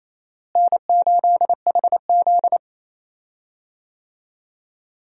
Morse code, heard as a keyed tone, about 28 wpm, 700 Hz. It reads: N8HZ
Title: USA Call Signs - 4 Characters - Repeated 3x - 28wpm